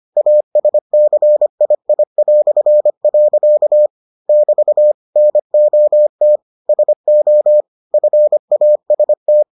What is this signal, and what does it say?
Morse code, heard as a keyed tone, about 25 wpm, 600 Hz. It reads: ASCII". <BT> NOT SO FAST